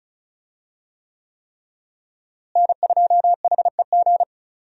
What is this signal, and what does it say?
Morse code, heard as a keyed tone, about 36 wpm, 700 Hz. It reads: N2HEG
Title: USA Call Signs - 5 Characters - Repeated 3x - 35wpm